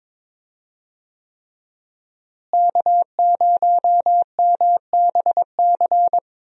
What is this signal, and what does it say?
Morse code, heard as a keyed tone, about 22 wpm, 700 Hz. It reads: K0MBC
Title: USA Call Signs - 5 Characters - Repeated 3x - 22wpm